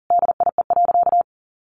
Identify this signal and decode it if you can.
Morse code, heard as a keyed tone, about 40 wpm, 700 Hz. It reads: DIE.
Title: Sentences from Top 1300 Words 40wpm